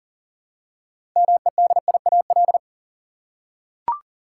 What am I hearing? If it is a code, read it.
Morse code, heard as a keyed tone, about 40 wpm, 700 Hz. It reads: MEDIAL
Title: Practice QSO - Exchange 3 - More Info - 40wpm